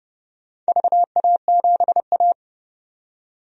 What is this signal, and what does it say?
Morse code, heard as a keyed tone, about 31 wpm, 700 Hz. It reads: VA7A
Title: DXCC-100 Call Signs - 4 Characters - 30wpm